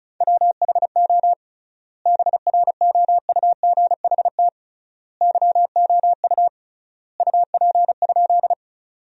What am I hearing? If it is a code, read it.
Morse code, heard as a keyed tone, about 34 wpm, 700 Hz. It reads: WHO BROUGHT YOU UP?
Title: 3x Word-Spacing - Sentences from Top 500 Words - 35wpm